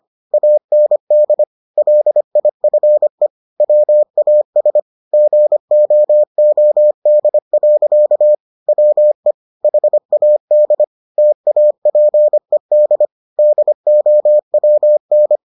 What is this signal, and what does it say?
Morse code, heard as a keyed tone, about 25 wpm, 600 Hz. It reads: AND LIFE WAS GOOD. WE HAD TAPED DOWN